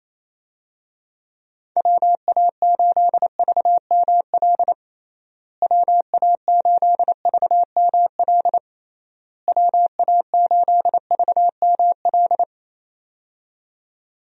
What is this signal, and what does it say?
Morse code, heard as a keyed tone, about 28 wpm, 700 Hz. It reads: WA8VML WA8VML WA8VML
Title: USA Call Signs - 6 Characters - Repeated 3x - 28wpm